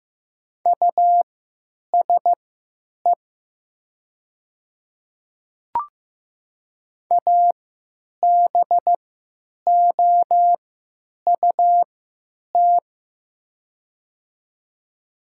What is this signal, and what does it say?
Morse code, heard as a keyed tone, about 15 wpm, spaced as for 8 wpm, 700 Hz. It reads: USE ABOUT
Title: 3x Character Spacing - Top 100 Words - 15wpm